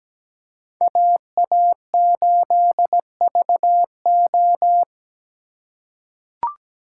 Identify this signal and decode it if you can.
Morse code, heard as a keyed tone, about 17 wpm, 700 Hz. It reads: AA8VO
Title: USA Call Signs - 5 Characters - 3x Character Spacing - 17wpm